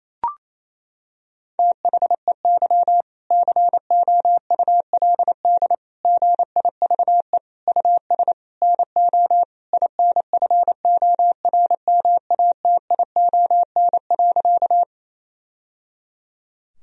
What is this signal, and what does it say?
Morse code, heard as a keyed tone, about 28 wpm, 700 Hz. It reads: THEY COULD GIVE US NO INFORMATION.
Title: Sentences from Top 1000 Words - Review Part 3 - 28wpm